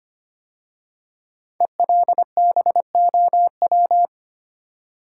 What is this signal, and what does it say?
Morse code, heard as a keyed tone, about 25 wpm, 700 Hz. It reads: ELBOW
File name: 5 Letter Words - First 3 Letters Given - 25wpm